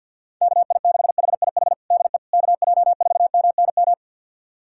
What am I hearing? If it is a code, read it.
Morse code, heard as a keyed tone, about 51 wpm, 700 Hz. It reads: KI6HIH DE KJ4MNM
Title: USA Callsign DE Callsign 50wpm